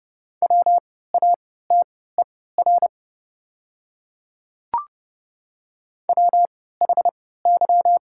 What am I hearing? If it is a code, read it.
Morse code, heard as a keyed tone, about 30 wpm, spaced as for 17 wpm, 690 Hz. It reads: WATER WHY
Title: 3x Character Spacing - Top 200 Words - 30wpm